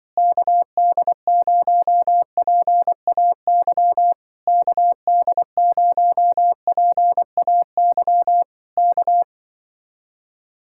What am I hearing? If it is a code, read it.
Morse code, heard as a keyed tone, about 24 wpm, 700 Hz. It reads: KD0PAY KD0PAY K
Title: Speed-Racing Common QSO Phrases Encore - 20wpm